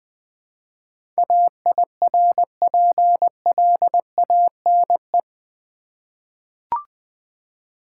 Morse code, heard as a keyed tone, about 20 wpm, 700 Hz. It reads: AIRPLANE
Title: Easy Words with Hints - 20wpm